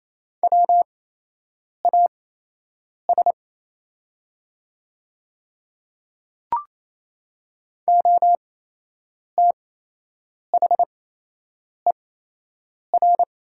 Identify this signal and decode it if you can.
Morse code, heard as a keyed tone, about 28 wpm, spaced as for 8 wpm, 700 Hz. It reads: WAS OTHER
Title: 8x Character Spacing - Top 100 Words - 28wpm